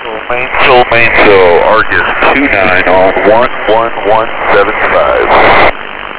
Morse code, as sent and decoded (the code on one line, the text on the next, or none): none